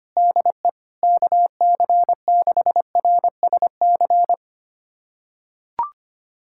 Morse code, heard as a keyed tone, about 25 wpm, 700 Hz. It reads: DE KC6RSC
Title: USA Callsign DE Callsign - 6 Characters - 25wpm